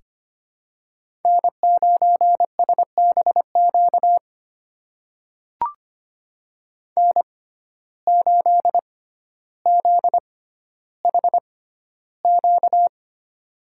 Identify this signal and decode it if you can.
Morse code, heard as a keyed tone, about 25 wpm, spaced as for 13 wpm, 700 Hz. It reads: N9SBQ N8ZHQ
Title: USA Call Signs - 5 Characters - 6x Character Spacing - 25wpm